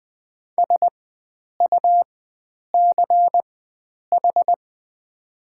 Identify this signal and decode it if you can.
Morse code, heard as a keyed tone, about 20 wpm, spaced as for 9 wpm, 700 Hz. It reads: SUCH
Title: 4x Character Spacing - Top 300 Words - 20wpm